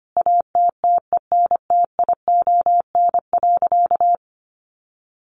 Morse code, heard as a keyed tone, about 25 wpm, 700 Hz. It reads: ATTENTION.
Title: Sentences from Top 1000 Words 25wpm